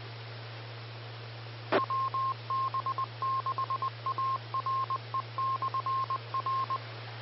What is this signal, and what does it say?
Morse code, heard as a keyed tone, about 20 wpm, 1060 Hz. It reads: WB6ARE/R
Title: Austin, TX Amateur Repeaters